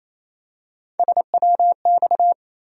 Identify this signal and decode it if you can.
Morse code, heard as a keyed tone, about 28 wpm, 700 Hz. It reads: SWX
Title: USA Call Sign Postfixes - 3 Letters 28wpm